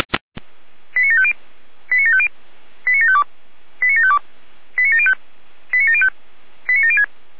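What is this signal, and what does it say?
Five-tone selective call (ZVEI), heard as quick runs of five tones, tones of about 70 ms. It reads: 89750 89750 89752 89752 89705 89705 89706
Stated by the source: ZVEI five-tone selective call received on an SDR.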